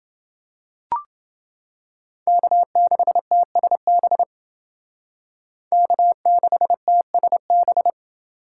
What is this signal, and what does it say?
Morse code, heard as a keyed tone, about 29 wpm, 700 Hz. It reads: K6TSB K6TSB